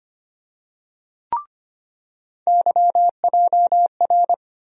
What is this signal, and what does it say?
Morse code, heard as a keyed tone, about 25 wpm, 700 Hz. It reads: YJR
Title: USA Call Sign Postfixes - 3 Letters 25wpm